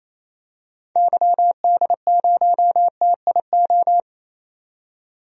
Morse code, heard as a keyed tone, about 28 wpm, 700 Hz. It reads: YD0TIO
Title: DXCC-100 Call Signs - 6 Characters - 28wpm